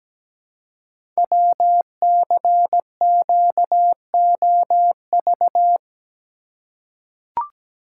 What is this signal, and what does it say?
Morse code, heard as a keyed tone, about 17 wpm, 700 Hz. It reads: WCQOV
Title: Code Groups - 5 Letters - 17wpm